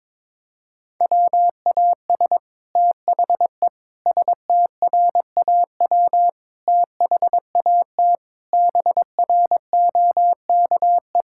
Morse code, heard as a keyed tone, about 21 wpm, 700 Hz. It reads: WAS THE STRAW THAT BROKE
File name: practice / 22wpm / Sets of 8 Words - 22wpm